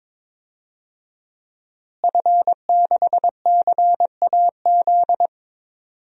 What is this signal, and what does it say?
Morse code, heard as a keyed tone, about 22 wpm, 700 Hz. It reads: F6CAZ
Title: DXCC-100 Call Signs - 5 Characters - Repeated 3x - 22wpm